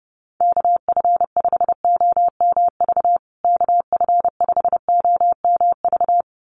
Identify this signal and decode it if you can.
Morse code, heard as a keyed tone, about 30 wpm, 700 Hz. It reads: KF5OMV KF5OMV
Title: USA Call Signs Repeated 2x 30wpm